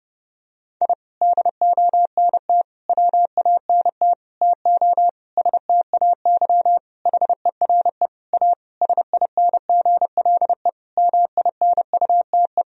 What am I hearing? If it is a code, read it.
Morse code, heard as a keyed tone, about 30 wpm, 700 Hz. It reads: I DONT WANT TO STAY HERE A SINGLE MINUTE
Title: Sentences from Top 1000 Words - Review Part 4 - 30wpm